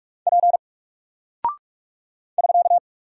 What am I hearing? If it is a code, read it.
Morse code, heard as a keyed tone, about 44 wpm, 700 Hz. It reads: P <SK>
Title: Single Character (Letter:Number:Punctuation:Prosign) Alt 45wpm